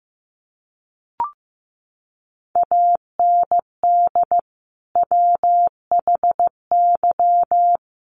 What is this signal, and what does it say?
Morse code, heard as a keyed tone, about 15 wpm, 700 Hz. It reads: AND WHY
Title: Sentences from Top 600 Words 15wpm